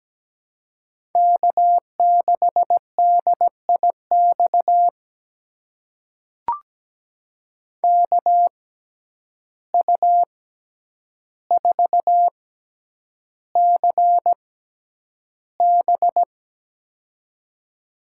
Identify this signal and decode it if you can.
Morse code, heard as a keyed tone, about 17 wpm, spaced as for 9 wpm, 700 Hz. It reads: K6DIX KU4CB
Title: USA Call Signs - 5 Characters - 6x Character Spacing - 17wpm